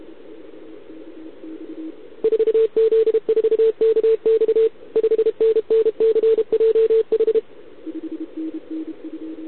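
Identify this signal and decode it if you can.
Morse code, heard as a keyed tone, about 32 wpm, 440 Hz. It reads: 4Z4KX 5NNCJH
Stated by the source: Contest CW from OK1RI on 80 m during the OKOM DX contest, recorded by a QRP station using a KX3 and a wire antenna.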